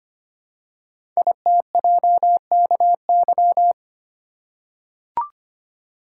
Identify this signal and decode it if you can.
Morse code, heard as a keyed tone, about 25 wpm, 700 Hz. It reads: ITJKY
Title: Code Groups - 5 Letters - 25wpm